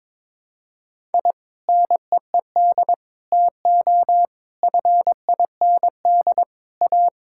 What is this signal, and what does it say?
Morse code, heard as a keyed tone, about 22 wpm, 700 Hz. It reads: I NEED TO FIND A
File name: practice / 22wpm / Sentences with 3-Word Phrases - 22wpm